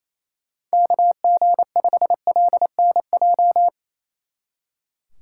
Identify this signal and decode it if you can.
Morse code, heard as a keyed tone, about 34 wpm, 700 Hz. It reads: KG5LNJ
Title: Speed Racing - USA Call Signs 28wpm